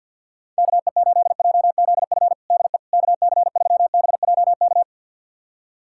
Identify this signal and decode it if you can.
Morse code, heard as a keyed tone, about 47 wpm, 700 Hz. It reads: KE8JZF DE KC3BJK
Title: USA Callsign DE Callsign - 6 Characters - 50wpm